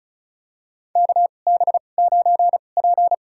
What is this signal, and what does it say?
Morse code, heard as a keyed tone, about 34 wpm, spaced as for 25 wpm, 700 Hz. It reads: KB9P
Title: USA Call Signs - 2x Character Spacing - 35wpm